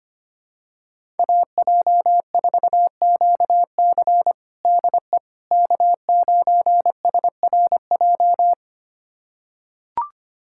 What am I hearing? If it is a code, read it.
Morse code, heard as a keyed tone, about 25 wpm, 700 Hz. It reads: AJ4QC DE K9SRJ
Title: USA Callsign DE Callsign - 5 Characters - 25wpm